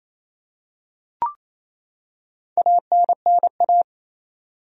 Morse code, heard as a keyed tone, about 28 wpm, 700 Hz. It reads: ANNA
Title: USA Common Names - Max 5 Letters - 28wpm